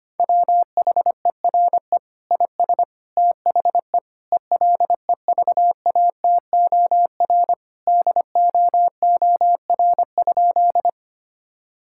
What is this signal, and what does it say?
Morse code, heard as a keyed tone, about 25 wpm, 700 Hz. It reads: WHERE IS THE ELEVATOR DOOR?
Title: Sentences from Top 1600 Words 25wpm